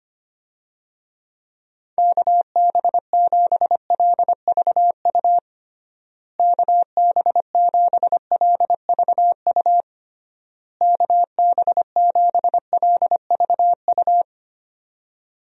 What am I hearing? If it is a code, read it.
Morse code, heard as a keyed tone, about 25 wpm, 700 Hz. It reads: KB7LVU KB7LVU KB7LVU
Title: USA Call Signs - 6 Characters - Repeated 3x - 25wpm